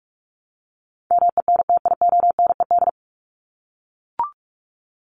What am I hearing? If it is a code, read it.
Morse code, heard as a keyed tone, about 44 wpm, 700 Hz. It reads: MENTIONED
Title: ED Suffix - Top 100 Words with Base Words 45wpm